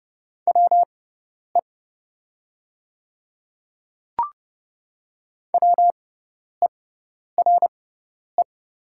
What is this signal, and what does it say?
Morse code, heard as a keyed tone, about 30 wpm, spaced as for 10 wpm, 690 Hz. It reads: WE WERE
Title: 6x Character Spacing - Top 100 Words - 30wpm